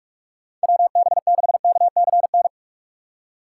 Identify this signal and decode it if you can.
Morse code, heard as a keyed tone, about 45 wpm, 700 Hz. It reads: WB6KCN
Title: USA Call Signs - 6 Characters - 3x Character Spacing - 45wpm